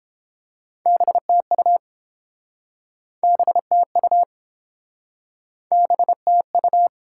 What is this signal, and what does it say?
Morse code, heard as a keyed tone, about 30 wpm, 700 Hz. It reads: BTU BTU BTU